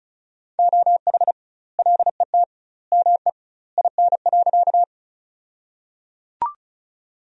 Morse code, heard as a keyed tone, about 34 wpm, 700 Hz. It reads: OH LET ME IN.
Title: 2x Word-Spacing - Sentences from Top 500 Words - 35wpm